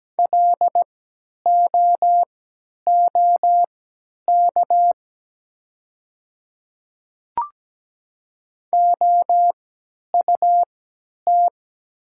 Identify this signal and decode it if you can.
Morse code, heard as a keyed tone, about 17 wpm, spaced as for 10 wpm, 700 Hz. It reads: LOOK OUT
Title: 3x Character Spacing - Top 200 Words - 17wpm